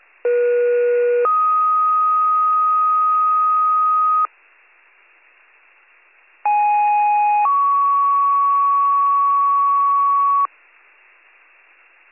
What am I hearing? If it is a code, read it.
Two-tone page, heard as a one-second tone, then a three-second tone, about 480 then 1250 Hz, about 830 then 1120 Hz. It is heard from 0.2 to 10.5 s.